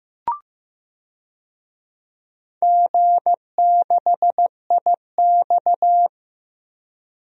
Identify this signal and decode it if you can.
Morse code, heard as a keyed tone, about 15 wpm, 700 Hz. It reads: G6IX
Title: DXCC-100 Call Signs - 4 Characters - 15wpm